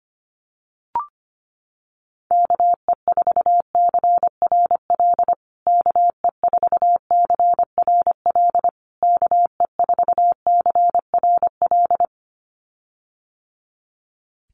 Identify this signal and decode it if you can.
Morse code, heard as a keyed tone, about 25 wpm, 700 Hz. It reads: KE4CRL KE4CRL KE4CRL
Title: USA Call Signs Repeated 3x 25wpm